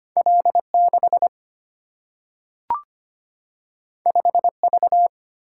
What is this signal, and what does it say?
Morse code, heard as a keyed tone, about 25 wpm, 700 Hz. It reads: L 6 5 V
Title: Single-Letter Single-Number 25wpm